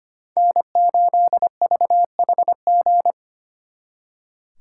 Morse code, heard as a keyed tone, about 30 wpm, 700 Hz. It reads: N8VHG
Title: Speed Racing - USA 1x3 Call Signs 25wpm